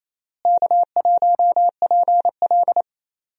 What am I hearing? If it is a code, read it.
Morse code, heard as a keyed tone, about 28 wpm, 700 Hz. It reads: K1PL
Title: USA Call Signs - 4 Characters - Repeated 2x - 28wpm